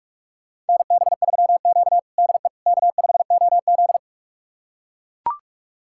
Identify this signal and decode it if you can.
Morse code, heard as a keyed tone, about 41 wpm, 700 Hz. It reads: NB3Q DE K5OZ
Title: USA Callsign DE Callsign - 4 Characters - 45wpm